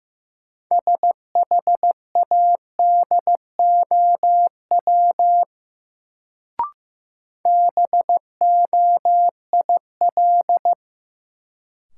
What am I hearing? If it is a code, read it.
Morse code, heard as a keyed tone, about 15 wpm, 700 Hz. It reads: SHADOW BOIL
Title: Sets of 5 Words Encore - New Word Drill - 15wpm